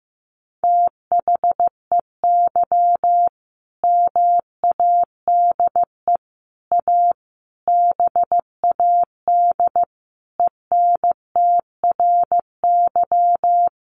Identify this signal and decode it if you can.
Morse code, heard as a keyed tone, about 15 wpm, 700 Hz. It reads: THEY MADE A BAD ENTRY
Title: Sentences from Top 1500 Words 15wpm